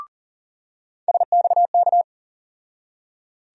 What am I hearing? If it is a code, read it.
Morse code, heard as a keyed tone, about 40 wpm, 700 Hz. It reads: SXK
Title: USA Call Sign Postfixes - 3 Letters 40wpm